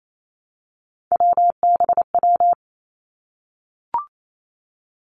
Morse code, heard as a keyed tone, about 28 wpm, 700 Hz. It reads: WBW
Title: Three Challenging Letter-Number Combinations 28wpm